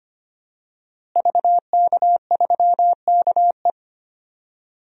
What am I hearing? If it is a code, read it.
Morse code, heard as a keyed tone, about 25 wpm, 700 Hz. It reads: VK3KE
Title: DXCC-100 Call Signs - 5 Characters - Repeated 3x - 25wpm